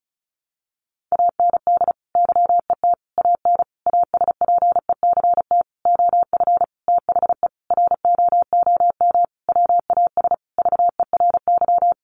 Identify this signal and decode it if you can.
Morse code, heard as a keyed tone, about 34 wpm, 700 Hz. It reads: AND YET AN ASPECT OF THE ROOM WAS VERY
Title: Sentences from Top 1300 Words 35wpm